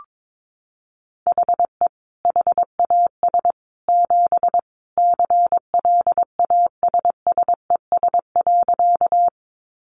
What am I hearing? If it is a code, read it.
Morse code, heard as a keyed tone, about 22 wpm, 700 Hz. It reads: HE HAS 7 CLASSES.
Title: Additional Easy Phrases 22wpm